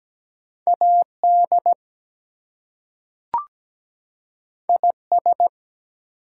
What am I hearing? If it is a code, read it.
Morse code, heard as a keyed tone, about 17 wpm, 700 Hz. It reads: AD IS
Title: Two Letter Words - 17wpm